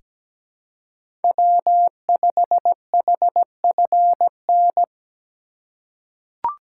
Morse code, heard as a keyed tone, about 17 wpm, 700 Hz. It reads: W5HFN
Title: USA Call Signs - 4x Character Spacing - 17wpm